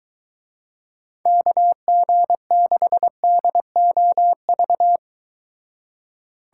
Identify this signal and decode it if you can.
Morse code, heard as a keyed tone, about 23 wpm, 700 Hz. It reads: KG6DOV